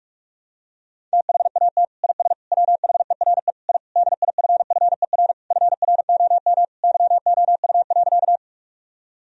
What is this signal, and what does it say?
Morse code, heard as a keyed tone, about 47 wpm, 700 Hz. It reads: THAT IS WHERE I DIFFER FROM YOU.
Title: Sentences from Top 1000 Words - Review Part 1 - 45wpm